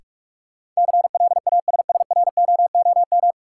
Morse code, heard as a keyed tone, about 46 wpm, 700 Hz. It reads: CLASSROOM
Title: Sentences from Top 1500 Words - New Word Drill Alt - 45wpm